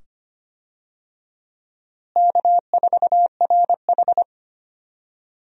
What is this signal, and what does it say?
Morse code, heard as a keyed tone, about 25 wpm, 700 Hz. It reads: K4RH
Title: USA Call Signs - 4 Characters - Repeated 2x - 25wpm